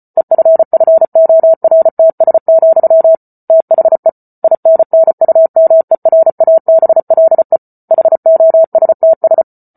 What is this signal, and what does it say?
Morse code, heard as a keyed tone, about 34 wpm, 650 Hz. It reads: EFFORTS, THE INNUMERABLE HOSTS